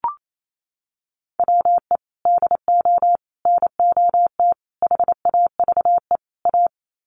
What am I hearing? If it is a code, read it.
Morse code, heard as a keyed tone, about 28 wpm, 700 Hz. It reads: WE DO NOT HAVE A
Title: Sets of 5 Words 28wpm